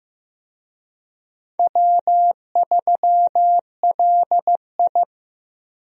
Morse code, heard as a keyed tone, about 15 wpm, 700 Hz. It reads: W3LI
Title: USA Call Signs - 4 Characters - 15wpm